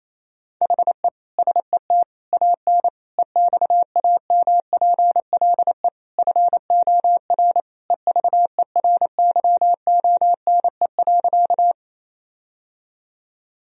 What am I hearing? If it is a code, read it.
Morse code, heard as a keyed tone, about 28 wpm, 700 Hz. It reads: HE SET AN EXAMPLE FOR EVERYONE.
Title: Sentences from Top 1000 Words - Review Part 1 - 28wpm